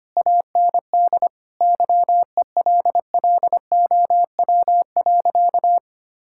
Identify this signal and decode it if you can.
Morse code, heard as a keyed tone, about 25 wpm, 690 Hz. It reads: AND YELLOW.
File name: Easy Sentences with Hints - 25wpm